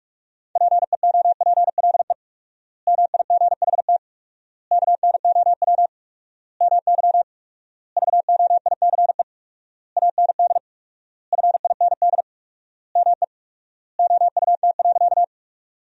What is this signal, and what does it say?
Morse code, heard as a keyed tone, about 46 wpm, 700 Hz. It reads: PEOPLE MIGHT KNOW MY VOICE AND FIND ME OUT.
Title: 4x Word-Spacing - Sentences from Top 500 Words - 45wpm